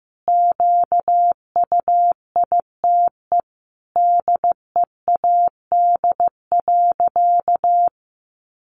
Morse code, heard as a keyed tone, about 15 wpm, 700 Hz. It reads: QUITE DEAD.
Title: Sentences from Top 1500 Words 15wpm